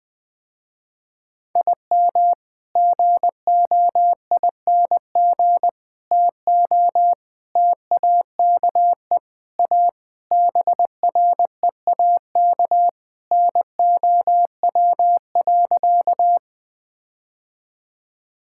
Morse code, heard as a keyed tone, about 20 wpm, 700 Hz. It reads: IM GOING TO TAKE A BREAK NOW.
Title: Sentences with 5-Word Phrases - Spoken Beginning - 20wpm